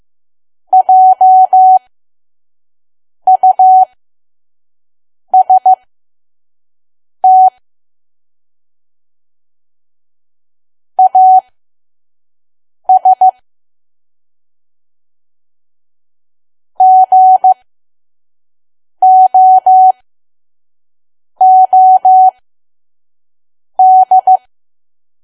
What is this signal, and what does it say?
Morse code, heard as a keyed tone, about 15 wpm, spaced as for 5 wpm, 750 Hz. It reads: JUST AS GOOD
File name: Archive / 5 WPM files / PM